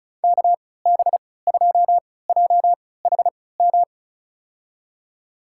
Morse code, heard as a keyed tone, about 36 wpm, spaced as for 20 wpm, 700 Hz. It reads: KB2JHM
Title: USA Call Signs - 6 Characters - 3x Character Spacing - 35wpm